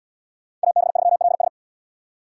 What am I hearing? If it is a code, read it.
Morse code, heard as a keyed tone, about 67 wpm, 700 Hz. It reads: AB3LS